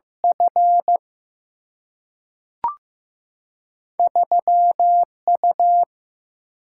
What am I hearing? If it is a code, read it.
Morse code, heard as a keyed tone, about 15 wpm, 700 Hz. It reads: F 3 U
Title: Single-Letter Single-Number 15wpm